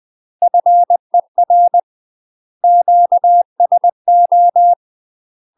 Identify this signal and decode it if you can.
Morse code, heard as a keyed tone, about 20 wpm, 700 Hz. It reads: FER QSO
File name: UK MASTER QSOs TRAINING / 20WPM / UK MASTER QSOs 0060